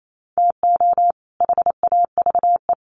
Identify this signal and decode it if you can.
Morse code, heard as a keyed tone, about 28 wpm, 700 Hz. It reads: TO HAVE
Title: Sets of 2 Words 28wpm